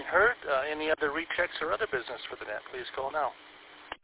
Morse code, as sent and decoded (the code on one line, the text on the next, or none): none